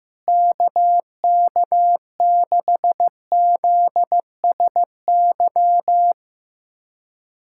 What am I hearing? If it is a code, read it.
Morse code, heard as a keyed tone, about 16 wpm, 700 Hz. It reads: KK6ZSY
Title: USA Callsign DE Callsign 15wpm